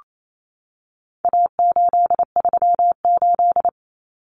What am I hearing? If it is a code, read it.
Morse code, heard as a keyed tone, about 28 wpm, 700 Hz. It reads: A838
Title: Common Cut Numbers - 4-Digit 28wpm